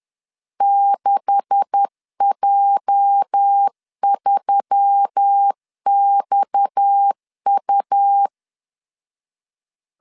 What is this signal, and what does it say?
Morse code, heard as a keyed tone, about 10 wpm, 800 Hz. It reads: 6J3XU
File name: CW / DET / 1 9 1500C S A 10wpm 4511